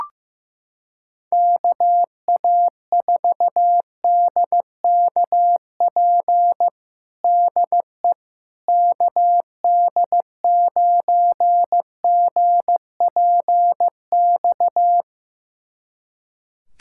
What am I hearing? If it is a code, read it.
Morse code, heard as a keyed tone, about 15 wpm, 700 Hz. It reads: KA4DKP DE KD9GPX
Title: USA Callsign DE Callsign 15wpm